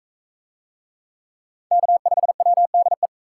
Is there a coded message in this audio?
Yes: Morse code from 1.7 to 3.1 s, heard as a keyed tone, about 42 wpm, 700 Hz. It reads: K5WDE